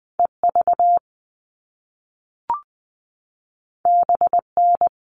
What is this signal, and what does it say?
Morse code, heard as a keyed tone, about 20 wpm, 700 Hz. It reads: EV BN
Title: All Cut Numbers - 2-Digit 20wpm